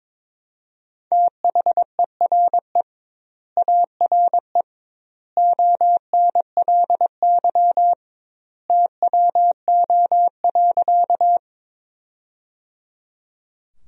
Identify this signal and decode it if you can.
Morse code, heard as a keyed tone, about 22 wpm, 700 Hz. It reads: THERE ARE ONLY TWO.
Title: 2x Word-Spacing - Sentences from Top 200 Words - 22wpm